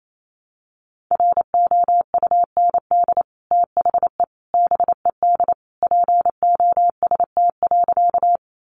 Morse code, heard as a keyed tone, about 28 wpm, 700 Hz. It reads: ROUND THE BED POST.
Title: Sentences from Top 1000 Words 28wpm